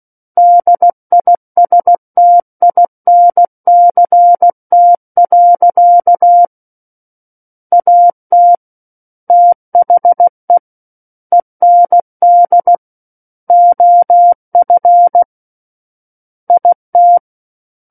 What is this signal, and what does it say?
Morse code, heard as a keyed tone, about 16 wpm, 700 Hz. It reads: DISTINCT. AT THE END OF IT
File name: Book One / 07-Chapter 7 - 16 WPM